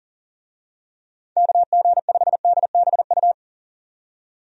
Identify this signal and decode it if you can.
Morse code, heard as a keyed tone, about 40 wpm, 700 Hz. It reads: KG5DBU
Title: USA Call Signs - 6 Characters - 40wpm